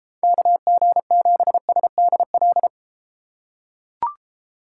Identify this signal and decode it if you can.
Morse code, heard as a keyed tone, about 29 wpm, 700 Hz. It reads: KG7SDL